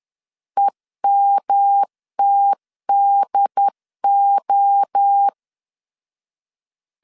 Morse code, heard as a keyed tone, about 10 wpm, 800 Hz. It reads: EMTDO
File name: CW / DET / 1 3 1500C S C 10wpm 3335